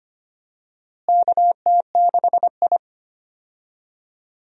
Morse code, heard as a keyed tone, about 25 wpm, 700 Hz. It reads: KT6I
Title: USA Call Signs - 4 Characters - Repeated 3x - 25wpm